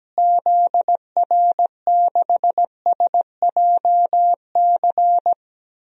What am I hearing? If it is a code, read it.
Morse code, heard as a keyed tone, about 17 wpm, 700 Hz. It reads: ZR6SJC
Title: DXCC-100 Call Signs - 6 Characters - Repeated 3x - 17wpm